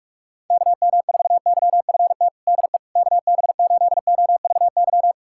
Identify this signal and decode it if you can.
Morse code, heard as a keyed tone, about 43 wpm, 700 Hz. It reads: KM4YFT DE KB8OVY
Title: USA Callsign DE Callsign - 6 Characters - 45wpm